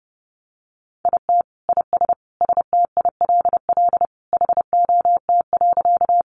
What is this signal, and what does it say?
Morse code, heard as a keyed tone, about 30 wpm, 700 Hz. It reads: IT IS STILL HOT.
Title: Easy Phrases 30wpm